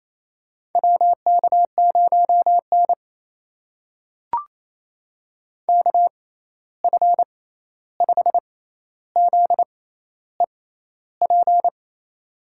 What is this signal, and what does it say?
Morse code, heard as a keyed tone, about 28 wpm, spaced as for 13 wpm, 700 Hz. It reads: WK0N KF5ZEP
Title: USA Call Signs - 6x Character Spacing - 28wpm